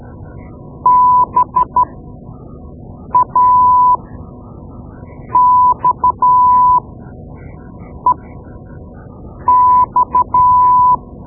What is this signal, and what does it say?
Morse code, heard as a keyed tone, about 12 wpm, spaced as for 5 wpm, 1000 Hz. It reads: BAXEX
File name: Morse Lessons / Morse Lessons / Tape-2B